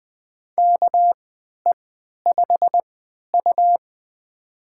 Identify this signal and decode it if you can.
Morse code, heard as a keyed tone, about 20 wpm, spaced as for 11 wpm, 700 Hz. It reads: KE5U
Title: USA Call Signs - 4 Characters - 3x Character Spacing - 20wpm